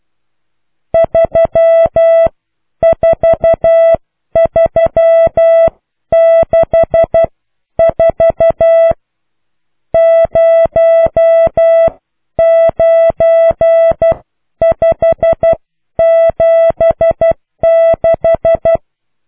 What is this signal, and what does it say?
Morse code, heard as a keyed tone, about 12 wpm, spaced as for 10 wpm, 660 Hz. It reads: 34364 09576